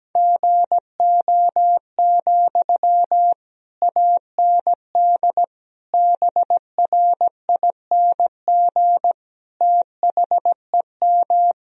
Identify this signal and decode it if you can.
Morse code, heard as a keyed tone, about 17 wpm, 700 Hz. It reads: GO, AND BRING THEM
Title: Sentences from Top 1000 Words - Review Part 3 - 17wpm